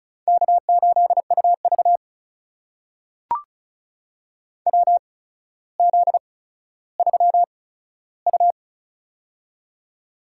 Morse code, heard as a keyed tone, about 36 wpm, spaced as for 15 wpm, 700 Hz. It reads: K8UV WZ3U
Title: USA Call Signs - 4 Characters - 8x Character Spacing - 35wpm